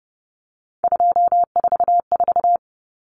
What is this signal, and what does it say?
Morse code, heard as a keyed tone, about 30 wpm, 700 Hz. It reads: 244
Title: Numbers - 3-Digit 30wpm